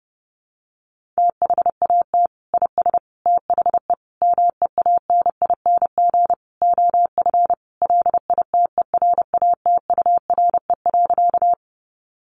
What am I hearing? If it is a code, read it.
Morse code, heard as a keyed tone, about 30 wpm, 700 Hz. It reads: THAT IS THE MEANING OF LITERATURE.
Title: Sentences from Top 900 Words 30wpm